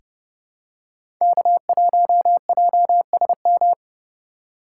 Morse code, heard as a keyed tone, about 32 wpm, 700 Hz. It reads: K1JSM
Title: USA Call Signs - 5 Characters - 2x Character Spacing - 30wpm